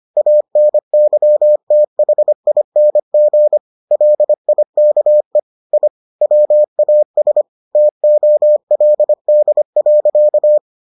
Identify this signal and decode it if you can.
Morse code, heard as a keyed tone, about 25 wpm, 600 Hz. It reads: ANYTHING LIKE I WAS TOLD.